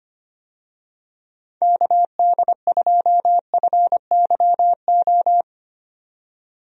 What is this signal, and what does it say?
Morse code, heard as a keyed tone, about 25 wpm, 700 Hz. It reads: KD2FYO
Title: USA Call Signs - 6 Characters - 25wpm